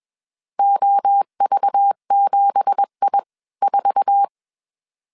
Morse code, heard as a keyed tone, about 21 wpm, spaced as for 18 wpm, 800 Hz. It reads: OV7I4